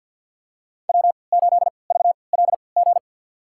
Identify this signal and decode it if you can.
Morse code, heard as a keyed tone, about 51 wpm, spaced as for 28 wpm, 700 Hz. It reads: W8VLG
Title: USA Call Signs - 5 Characters - 3x Character Spacing - 50wpm